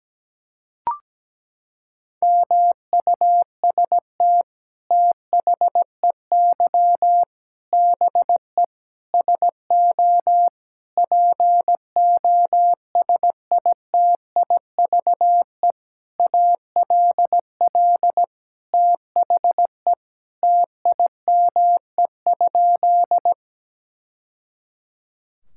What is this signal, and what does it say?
Morse code, heard as a keyed tone, about 17 wpm, 700 Hz. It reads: MUST THEY BE SO POSITIVE ALL THE TIME?
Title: Sentences from Top 1000 Words - Review Part 5 - 17wpm